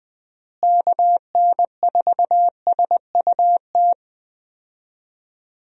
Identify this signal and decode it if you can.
Morse code, heard as a keyed tone, about 20 wpm, 700 Hz. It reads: KN4SUT
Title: USA Call Signs - 6 Characters - Repeated 2x - 20wpm